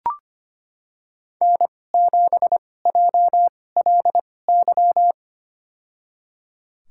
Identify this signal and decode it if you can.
Morse code, heard as a keyed tone, about 25 wpm, spaced as for 18 wpm, 700 Hz. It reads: N7JLY
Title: USA Call Signs - 2x Character Spacing - 25wpm